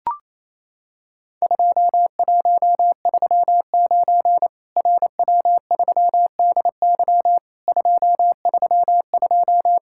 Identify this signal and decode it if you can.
Morse code, heard as a keyed tone, about 27 wpm, 700 Hz. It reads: 2139 RW3DY 232
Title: Worked All Europe DX Contest - QTC Exchange - 28wpm